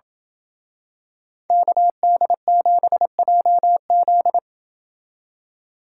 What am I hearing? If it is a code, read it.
Morse code, heard as a keyed tone, about 27 wpm, 700 Hz. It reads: KD7JZ